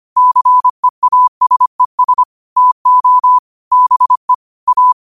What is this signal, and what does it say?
Morse code, heard as a keyed tone, about 25 wpm, 1000 Hz. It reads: CEASES TO BE A
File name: qotdfiles / 25wpm